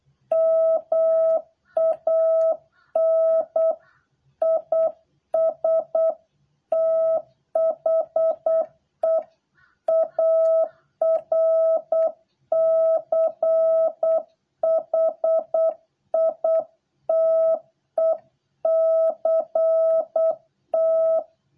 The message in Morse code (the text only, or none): MAN IS THE ARCHITECT